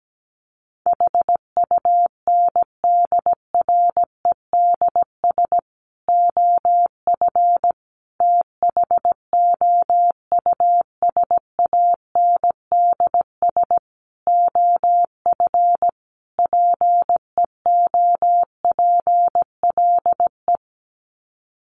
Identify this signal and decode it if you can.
Morse code, heard as a keyed tone, about 17 wpm, 700 Hz. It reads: HUNDREDS OF THOUSANDS OF PEOPLE
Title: Sets of 5 Words Encore2 17wpm